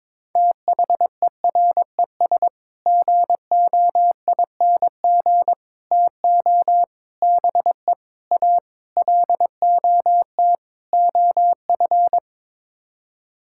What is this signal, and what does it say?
Morse code, heard as a keyed tone, about 22 wpm, 700 Hz. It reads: THERES GOING TO BE A LOT OF
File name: practice / 22wpm / Sets of 7 Words - 22wpm